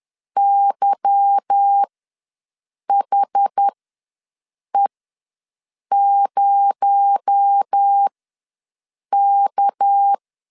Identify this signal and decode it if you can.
Morse code, heard as a keyed tone, about 10 wpm, spaced as for 6 wpm, 800 Hz. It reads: YHE0K
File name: LEZIONI / 9 286 ws 10wpm 4032